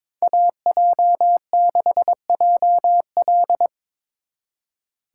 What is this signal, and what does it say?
Morse code, heard as a keyed tone, about 24 wpm, 700 Hz. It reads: AJ6JL
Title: Speed Racing - USA 2x2 Call Signs 22wpm